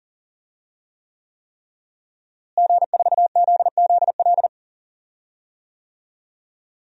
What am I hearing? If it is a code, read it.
Morse code, heard as a keyed tone, about 40 wpm, 700 Hz. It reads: G4ZZL
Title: DXCC-100 Call Signs - 5 Characters - Repeated 3x - 40wpm